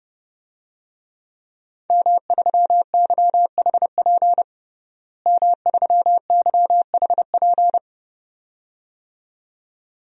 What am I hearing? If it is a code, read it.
Morse code, heard as a keyed tone, about 32 wpm, 700 Hz. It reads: M3YHP M3YHP
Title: DXCC-100 Call Signs - 5 Characters - Repeated 2x - 30wpm